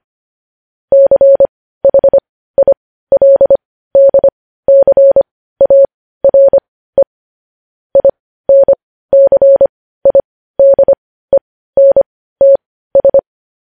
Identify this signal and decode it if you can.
Morse code, heard as a keyed tone, about 25 wpm, spaced as for 15 wpm, 550 Hz. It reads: CHILDCARE INCIDENTS
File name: podcast / News headlines-1x-25x15